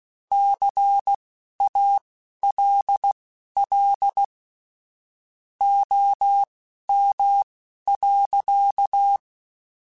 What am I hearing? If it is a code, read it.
Morse code, heard as a keyed tone, about 16 wpm, spaced as for 11 wpm, 800 Hz. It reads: CALL OM.
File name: QSOs / QSOs-mp3s / kc7sac@10